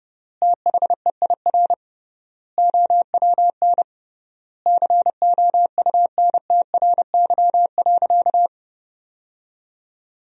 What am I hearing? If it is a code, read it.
Morse code, heard as a keyed tone, about 30 wpm, 700 Hz. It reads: THEIR OWN COUNTRY.
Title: 3x Word-Spacing - Sentences from Top 300 Words - 30wpm